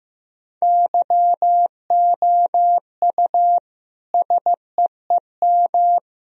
Morse code, heard as a keyed tone, about 15 wpm, 700 Hz. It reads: YOU SEEM
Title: Sentences from Top 1000 Words - Review Part 1 - 15wpm